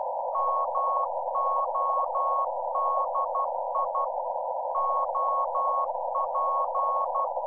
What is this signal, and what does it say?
Morse code, heard as a keyed tone, about 12 wpm, 1100 Hz. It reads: MODI OP
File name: Z PI4VRZ / PI4VRZ 14-jun-2025